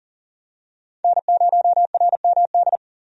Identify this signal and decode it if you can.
Morse code, heard as a keyed tone, about 40 wpm, 700 Hz. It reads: N0RMD
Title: USA Call Signs - 5 Characters - Repeated 3x - 40wpm